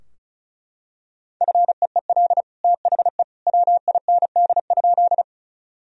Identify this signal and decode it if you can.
Morse code, heard as a keyed tone, about 36 wpm, 700 Hz. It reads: FEEL THE WIND?
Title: Sentences from Top 1000 Words - Review Part 2 - 35wpm